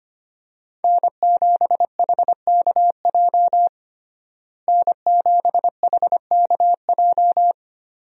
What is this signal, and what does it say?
Morse code, heard as a keyed tone, about 25 wpm, 700 Hz. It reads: N7HKJ N7HKJ
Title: USA Call Signs - 5 Characters - Repeated 3x - 25wpm